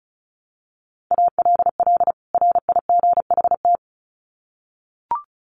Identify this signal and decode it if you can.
Morse code, heard as a keyed tone, about 34 wpm, 700 Hz. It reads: ALL RIGHT
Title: Sets of 2 Words 35wpm